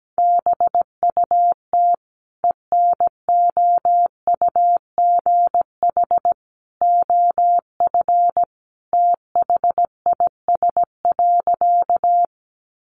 Morse code, heard as a keyed tone, about 17 wpm, 700 Hz. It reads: BUT ENOUGH OF THIS.
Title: Sentences from Top 400 Words 17wpm